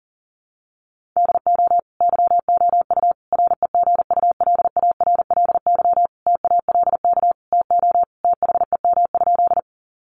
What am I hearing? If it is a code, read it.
Morse code, heard as a keyed tone, about 40 wpm, 700 Hz. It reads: DO YOU REGULARLY TALK TO THEM?
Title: Sentences from Top 1400 Words 40wpm